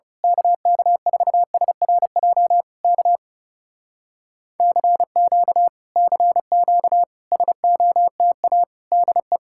Morse code, heard as a keyed tone, about 28 wpm, 700 Hz. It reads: KK4SRJ K CQ CQ SOTA DE
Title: Speed-Racing Common QSO Phrases Encore - 30wpm